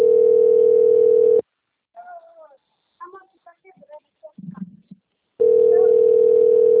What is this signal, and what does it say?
Ringback tone (North America), 440 + 480 Hz.
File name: out- - - - - .73458